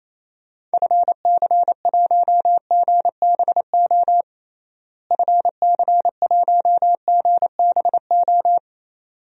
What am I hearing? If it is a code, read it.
Morse code, heard as a keyed tone, about 28 wpm, 700 Hz. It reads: FC1GBO FC1GBO
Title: DXCC-100 Call Signs - 6 Characters - Repeated 2x - 28wpm